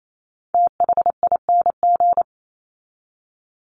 Morse code, heard as a keyed tone, about 28 wpm, 700 Hz. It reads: THING
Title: Top 300 Words 28wpm